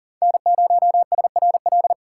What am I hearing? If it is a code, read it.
Morse code, heard as a keyed tone, about 40 wpm, 700 Hz. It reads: N0SRL
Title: USA Call Signs - 5 Characters - 4x Character Spacing - 40wpm